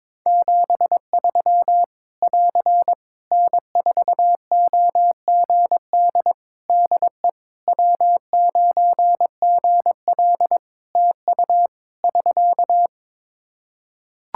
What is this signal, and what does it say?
Morse code, heard as a keyed tone, about 22 wpm, 700 Hz. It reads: 73 <AR> N4OGD DE W9GL TU <SK>
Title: K1USN SST QSO Practice 22wpm